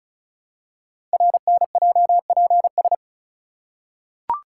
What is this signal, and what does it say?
Morse code, heard as a keyed tone, about 36 wpm, 700 Hz. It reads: RNJPS
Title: Code Groups - 5 Letters - 35wpm